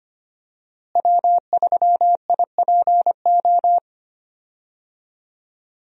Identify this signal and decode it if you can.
Morse code, heard as a keyed tone, about 25 wpm, 700 Hz. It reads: W3IPO
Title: USA Call Signs - 5 Characters - Repeated 3x - 25wpm